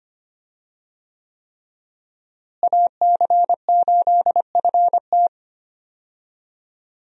Morse code, heard as a keyed tone, about 25 wpm, 700 Hz. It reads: AC8FT
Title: USA Call Signs - 5 Characters - Repeated 3x - 25wpm